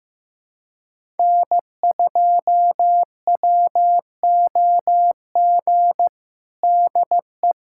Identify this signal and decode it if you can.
Morse code, heard as a keyed tone, about 15 wpm, 700 Hz. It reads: N2WOG DE
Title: USA Callsign DE Callsign - 5 Characters - 15wpm